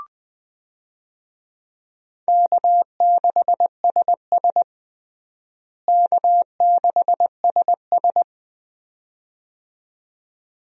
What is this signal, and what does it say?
Morse code, heard as a keyed tone, about 20 wpm, 700 Hz. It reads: K6SS K6SS
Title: USA Call Signs - 4 Characters - Repeated 2x - 20wpm